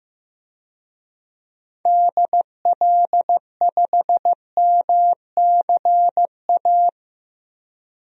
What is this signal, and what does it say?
Morse code, heard as a keyed tone, about 15 wpm, 700 Hz. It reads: DL5MCA
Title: DXCC-100 Call Signs - 6 Characters - Repeated 2x - 15wpm